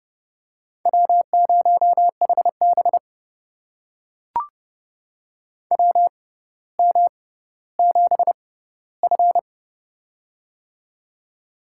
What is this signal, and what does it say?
Morse code, heard as a keyed tone, about 30 wpm, spaced as for 15 wpm, 700 Hz. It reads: W0HB WM7F
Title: USA Call Signs - 4 Characters - 6x Character Spacing - 30wpm